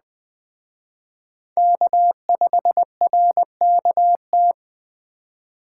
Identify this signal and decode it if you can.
Morse code, heard as a keyed tone, about 20 wpm, 700 Hz. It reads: K5RKT
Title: Speed Racing - USA 1x3 Call Signs 17wpm